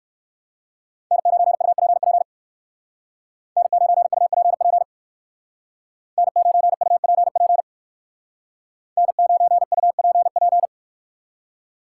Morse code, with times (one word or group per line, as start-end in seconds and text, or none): 1.1-2.2 N9UPP
3.6-4.8 N9UPP
6.2-7.6 N9UPP
9.0-10.7 N9UPP